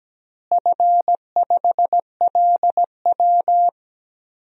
Morse code, heard as a keyed tone, about 17 wpm, 700 Hz. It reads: F5LW
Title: DXCC-100 Call Signs - 4 Characters - Repeated 3x - 17wpm